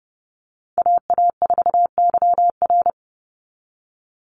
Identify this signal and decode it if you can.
Morse code, heard as a keyed tone, about 30 wpm, 700 Hz. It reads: AA4YR
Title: USA Call Signs 30wpm